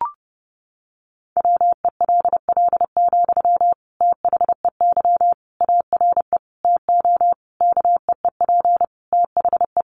Morse code, heard as a keyed tone, about 30 wpm, 700 Hz. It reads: WELL, THEY ARE TO KEEP THE
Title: Sentences from Top 1500 Words 30wpm